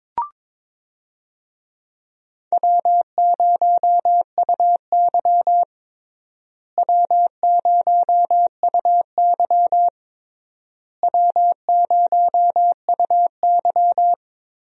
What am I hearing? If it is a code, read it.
Morse code, heard as a keyed tone, about 22 wpm, 700 Hz. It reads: W0UY W0UY W0UY
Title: USA Call Signs - 4 Characters - Repeated 3x - 22wpm